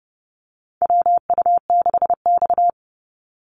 Morse code, heard as a keyed tone, about 31 wpm, 700 Hz. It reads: WU6X
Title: Call Signs - 30wpm